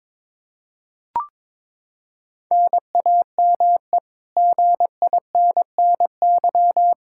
Morse code, heard as a keyed tone, about 22 wpm, 700 Hz. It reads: NAME GINNY
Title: Practice QSO - Exchange 1 - Beginning - 22wpm